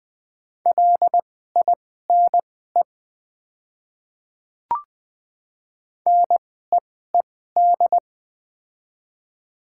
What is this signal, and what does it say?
Morse code, heard as a keyed tone, about 20 wpm, spaced as for 14 wpm, 700 Hz. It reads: LINE NEED
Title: 2x Character Spacing - Top 200 Words - 20wpm